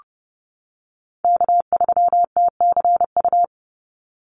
Morse code, heard as a keyed tone, about 30 wpm, 700 Hz. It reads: K3TCU
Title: NA SOTA Chaser Callsigns 30wpm